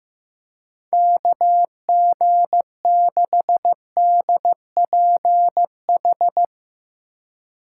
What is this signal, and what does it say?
Morse code, heard as a keyed tone, about 15 wpm, 700 Hz. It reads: KG6DPH
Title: USA Call Signs - 6 Characters - 4x Character Spacing - 15wpm